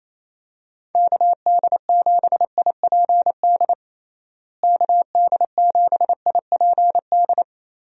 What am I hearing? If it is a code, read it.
Morse code, heard as a keyed tone, about 27 wpm, 700 Hz. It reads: KD7IPD KD7IPD
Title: USA Call Signs - 6 Characters - Repeated 2x - 28wpm